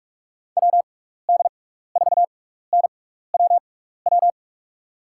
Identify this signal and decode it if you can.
Morse code, heard as a keyed tone, about 44 wpm, spaced as for 15 wpm, 700 Hz. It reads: WD4NWW
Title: USA Call Signs - 6 Characters - 6x Character Spacing - 45wpm